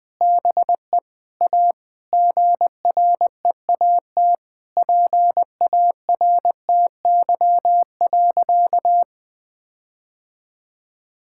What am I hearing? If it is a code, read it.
Morse code, heard as a keyed tone, about 21 wpm, 700 Hz. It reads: BE A GREAT PARTY.
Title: Sentences with 5-Word Phrases - 20wpm